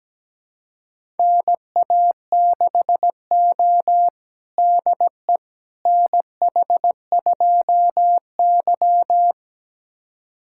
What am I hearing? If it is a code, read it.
Morse code, heard as a keyed tone, about 17 wpm, 700 Hz. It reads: NA6O DE NH2Y
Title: USA Callsign DE Callsign - 4 Characters - 17wpm